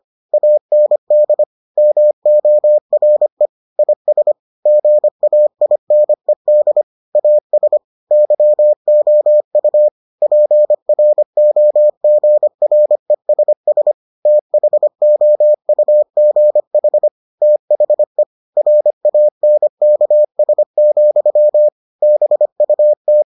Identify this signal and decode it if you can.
Morse code, heard as a keyed tone, about 25 wpm, 600 Hz. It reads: AND MORE IS GAINED AS YOU PROGRESS THOUGH THE RANKS, BUT